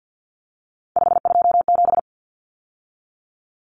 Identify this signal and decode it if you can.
Morse code, heard as a keyed tone, about 50 wpm, 700 Hz. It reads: 527
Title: Numbers - 3-Digit 50wpm